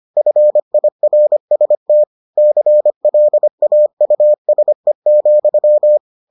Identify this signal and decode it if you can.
Morse code, heard as a keyed tone, about 25 wpm, 600 Hz. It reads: FIRST CLAUSE,